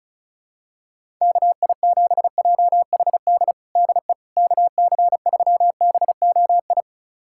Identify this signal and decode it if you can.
Morse code, heard as a keyed tone, about 34 wpm, 700 Hz. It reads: KI7JHD DE KC3BOI
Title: USA Callsign DE Callsign 35wpm